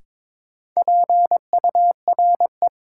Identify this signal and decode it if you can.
Morse code, heard as a keyed tone, about 22 wpm, 700 Hz. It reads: PURE
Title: Sentences from Top 1500 Words - New Word Drill - 22wpm